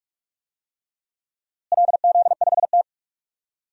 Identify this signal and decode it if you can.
Morse code, heard as a keyed tone, about 48 wpm, 700 Hz. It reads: LZ5T
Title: DXCC-100 Call Signs - 4 Characters - Repeated 3x - 45wpm